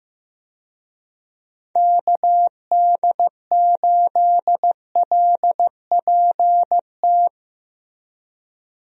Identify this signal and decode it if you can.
Morse code, heard as a keyed tone, about 16 wpm, 700 Hz. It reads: KD8LPT
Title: USA Call Signs - 6 Characters - 15wpm